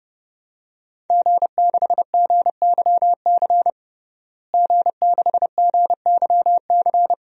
Morse code, heard as a keyed tone, about 30 wpm, 700 Hz. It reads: G6GYC G6GYC
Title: DXCC-100 Call Signs - 5 Characters - Repeated 2x - 30wpm